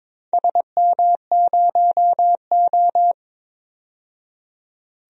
Morse code, heard as a keyed tone, about 22 wpm, 700 Hz. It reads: SM0O
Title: DXCC-100 Call Signs - 4 Characters - 22wpm